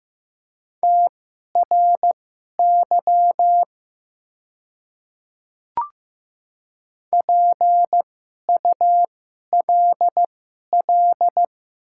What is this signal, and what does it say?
Morse code, heard as a keyed tone, about 15 wpm, spaced as for 11 wpm, 690 Hz. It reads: TRY PULL
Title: 2x Character Spacing - Top 500 Words - 15wpm